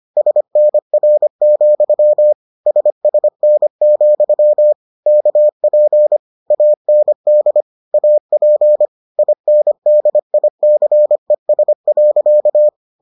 Morse code, heard as a keyed tone, about 25 wpm, 600 Hz. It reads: SNR, SSN, KP AND AP INDICES.